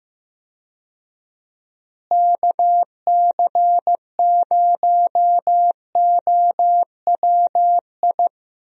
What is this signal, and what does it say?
Morse code, heard as a keyed tone, about 15 wpm, 700 Hz. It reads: KC0OWI
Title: USA Call Signs - 6 Characters - Repeated 3x - 15wpm